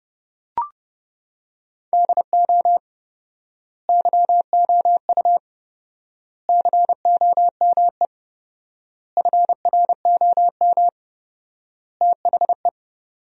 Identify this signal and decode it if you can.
Morse code, heard as a keyed tone, about 29 wpm, 700 Hz. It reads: DO YOU COME FROM THE
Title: 4x Word-Spacing - Sentences from Top 200 Words - 30wpm